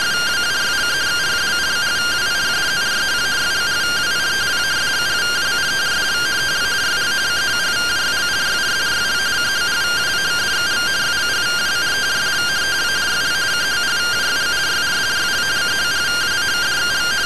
Two-tone teleprinter signal, about 170 Hz shift.
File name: findings / CT AXIAL -256.gen-04 rtty rx